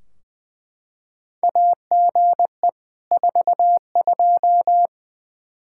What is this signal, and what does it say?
Morse code, heard as a keyed tone, about 20 wpm, 700 Hz. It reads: AGE 42
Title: Practice QSO - Exchange 3 - More Info - 20wpm